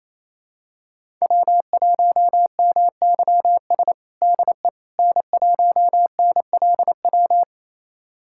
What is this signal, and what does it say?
Morse code, heard as a keyed tone, about 27 wpm, 700 Hz. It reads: W1MYS DE N1NLW
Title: USA Callsign DE Callsign - 5 Characters - 28wpm